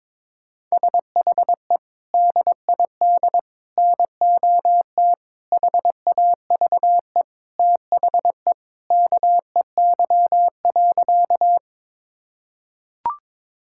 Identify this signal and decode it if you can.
Morse code, heard as a keyed tone, about 22 wpm, 700 Hz. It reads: SHE DID NOT HAVE THE KEY.
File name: practice / 22wpm / Sentences with 4-Word Phrases - 22wpm